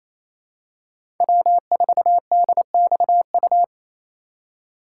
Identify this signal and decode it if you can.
Morse code, heard as a keyed tone, about 28 wpm, 700 Hz. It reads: W4DXU
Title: USA Call Signs - 5 Characters - 2x Character Spacing - 28wpm